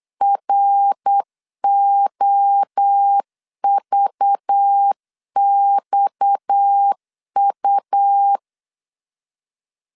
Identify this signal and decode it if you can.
Morse code, heard as a keyed tone, about 8 wpm, 800 Hz. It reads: ROVXU
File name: CW / DET / 1 6 1500C S B 8wpm 5142